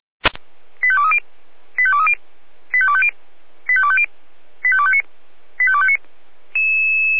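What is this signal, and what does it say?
Five-tone selective call (ZVEI), heard as quick runs of five tones, tones of about 70 ms. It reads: 85230 85230 85250 85250 85259 85259
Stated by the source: ZVEI five-tone call received by SDR, archived by Feuerwehr Orlamünde.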